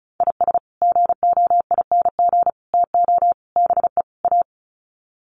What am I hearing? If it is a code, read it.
Morse code, heard as a keyed tone, about 34 wpm, 700 Hz. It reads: IS GOING TO BE A
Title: Sets of 5 Words 35wpm